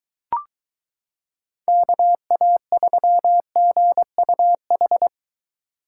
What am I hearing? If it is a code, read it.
Morse code, heard as a keyed tone, about 22 wpm, 700 Hz. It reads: KA3GUH